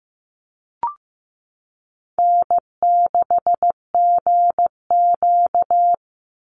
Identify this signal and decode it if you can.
Morse code, heard as a keyed tone, about 15 wpm, 700 Hz. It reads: N6GQ
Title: Easy Call Signs Repeated 3x - 15wpm